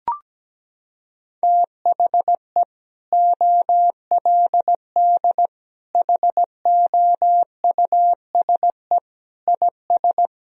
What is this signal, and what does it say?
Morse code, heard as a keyed tone, about 17 wpm, 700 Hz. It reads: THE OLD HOUSE IS
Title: Sentences from Top 1000 Words - Review Part 1 - 17wpm